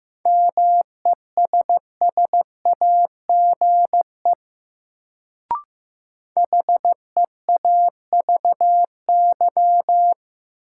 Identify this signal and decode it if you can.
Morse code, heard as a keyed tone, about 15 wpm, 700 Hz. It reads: MESSAGE HEAVY
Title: Sentences from Top 1100 Words - New Word Drill Alt - 15wpm